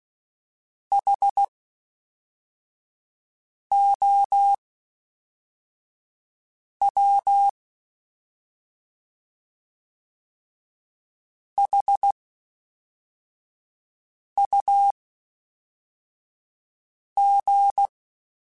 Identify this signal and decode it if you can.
Morse code, heard as a keyed tone, about 16 wpm, spaced as for 4 wpm, 800 Hz. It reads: HOW HUG
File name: H Prefixes / H mp3s